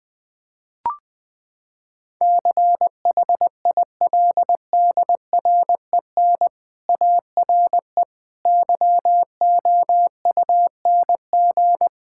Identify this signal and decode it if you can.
Morse code, heard as a keyed tone, about 20 wpm, 700 Hz. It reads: CHILDREN ARE YOUNG
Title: Sentences from Top 1000 Words - Review Part 6 - 20wpm